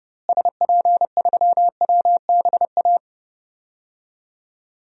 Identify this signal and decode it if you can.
Morse code, heard as a keyed tone, about 30 wpm, 700 Hz. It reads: SP3WBA
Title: DXCC-100 Call Signs - 6 Characters - Repeated 2x - 30wpm